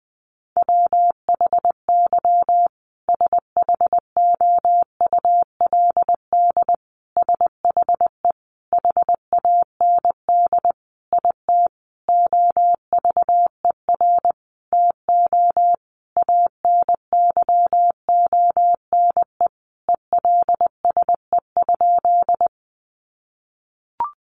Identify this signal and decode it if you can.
Morse code, heard as a keyed tone, about 20 wpm, 700 Hz. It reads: WHY SHOULD SHE HAND IT OVER TO ANYONE ELSE?
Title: Sentences from Top 900 Words 20wpm